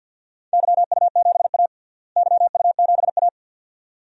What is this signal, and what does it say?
Morse code, heard as a keyed tone, about 54 wpm, 700 Hz. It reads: YU7A YU7A
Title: DXCC-100 Call Signs - 4 Characters - Repeated 2x - 50wpm